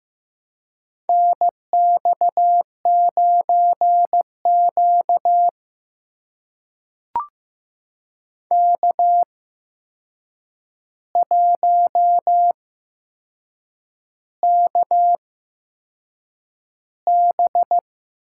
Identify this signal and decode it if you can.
Morse code, heard as a keyed tone, about 15 wpm, spaced as for 6 wpm, 700 Hz. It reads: NX9Q K1KB
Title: USA Call Signs - 4 Characters - 8x Character Spacing - 15wpm